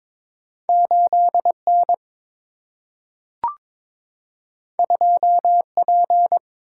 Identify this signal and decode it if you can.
Morse code, heard as a keyed tone, about 22 wpm, 700 Hz. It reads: 8 N 2 P
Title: Single-Letter Single-Number 22wpm